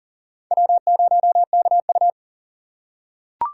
Morse code, heard as a keyed tone, about 40 wpm, 700 Hz. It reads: W0KU
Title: USA Call Signs - 4 Characters - 3x Character Spacing - 40wpm